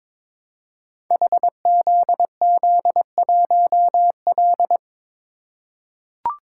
Morse code, heard as a keyed tone, about 22 wpm, 700 Hz. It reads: HZZ1L
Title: Code Groups - 5 Letter:Numbers - 22wpm